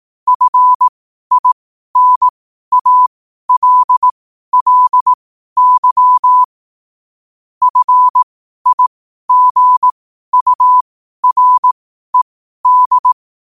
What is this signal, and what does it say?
Morse code, heard as a keyed tone, about 18 wpm, spaced as for 12 wpm, 1000 Hz. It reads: FINALLY FIGURED
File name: qotdfiles / 12wpm